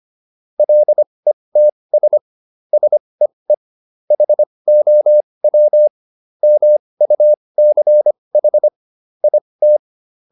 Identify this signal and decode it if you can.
Morse code, heard as a keyed tone, about 25 wpm, spaced as for 20 wpm, 600 Hz. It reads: LETS SEE HOW MUCH IT